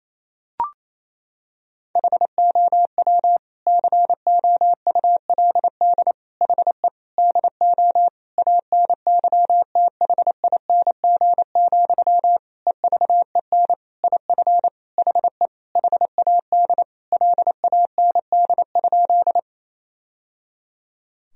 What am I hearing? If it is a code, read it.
Morse code, heard as a keyed tone, about 28 wpm, 700 Hz. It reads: HOW COULD HE DO ANYTHING, EVEN IF HE HAD LAND?
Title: Sentences from Top 1000 Words - Review Part 1 - 28wpm